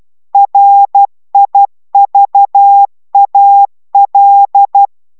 Morse code, heard as a keyed tone, about 12 wpm, 800 Hz. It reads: RIVAL